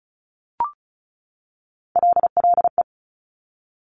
Morse code, heard as a keyed tone, about 36 wpm, 700 Hz. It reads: LLE
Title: Top Three Letter Combinations in Words 35wpm